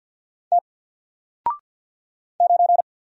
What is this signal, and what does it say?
Morse code, heard as a keyed tone, about 48 wpm, 700 Hz. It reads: T 9
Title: Single Letter-Number Alt 50wpm